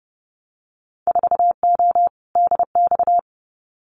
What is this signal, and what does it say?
Morse code, heard as a keyed tone, about 30 wpm, 700 Hz. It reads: 4O DX
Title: Winter Field Day Exchanges Encore 30wpm